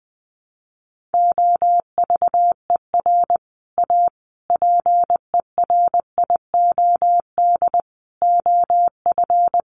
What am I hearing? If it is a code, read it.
Morse code, heard as a keyed tone, about 20 wpm, 700 Hz. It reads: OVER A PERIOD OF
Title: Sets of 5 Words Encore2 20wpm